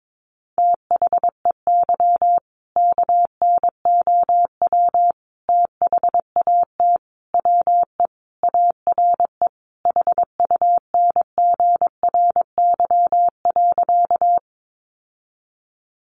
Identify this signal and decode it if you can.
Morse code, heard as a keyed tone, about 22 wpm, 700 Hz. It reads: THEY KNOW THAT WE ARE HUNGRY.
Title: Sentences from Top 1400 Words 22wpm